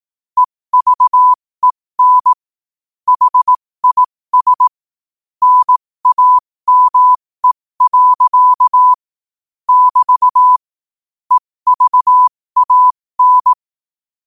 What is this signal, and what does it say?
Morse code, heard as a keyed tone, about 18 wpm, spaced as for 15 wpm, 1000 Hz. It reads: EVEN HIS NAME. <BT> EVAN
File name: qotdfiles / 15wpm